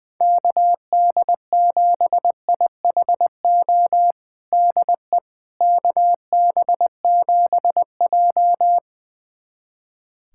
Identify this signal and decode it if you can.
Morse code, heard as a keyed tone, about 20 wpm, 700 Hz. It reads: KD7IHO DE KB7J
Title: USA Callsign DE Callsign 20wpm